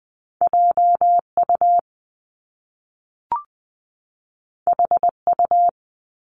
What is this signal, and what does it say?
Morse code, heard as a keyed tone, about 20 wpm, 700 Hz. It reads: JU HU
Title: Top Two Letter Combinations in Words 20wpm